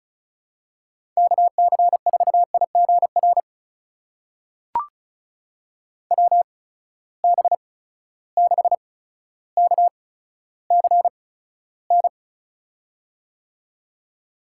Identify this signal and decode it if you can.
Morse code, heard as a keyed tone, about 36 wpm, spaced as for 15 wpm, 700 Hz. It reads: KC4IGR WB6KCN
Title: USA Call Signs - 6 Characters - 8x Character Spacing - 35wpm